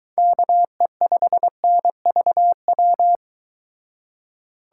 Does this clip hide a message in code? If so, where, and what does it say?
Yes: Morse code from 0.2 to 3.2 s, heard as a keyed tone, about 23 wpm, 700 Hz. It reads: KE5NVW